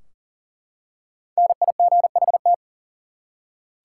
Morse code, heard as a keyed tone, about 40 wpm, 700 Hz. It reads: NIGHT
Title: Five Letter Words - 40wpm